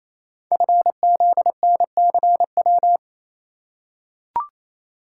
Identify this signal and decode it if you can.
Morse code, heard as a keyed tone, about 27 wpm, 700 Hz. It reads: FZNCW
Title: Code Groups - 5 Letters - 28wpm